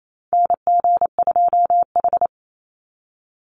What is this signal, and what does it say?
Morse code, heard as a keyed tone, about 28 wpm, 700 Hz. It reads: NG2H
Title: Call Signs - 28wpm